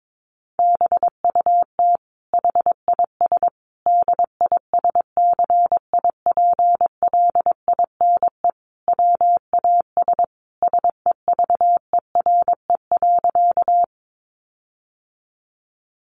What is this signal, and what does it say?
Morse code, heard as a keyed tone, about 22 wpm, 700 Hz. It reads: BUT HIS DISCIPLINE WAS SEVERE.
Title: Sentences from Top 1400 Words 22wpm